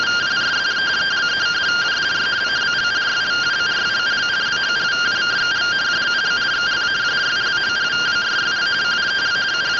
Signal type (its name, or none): two-tone teleprinter signal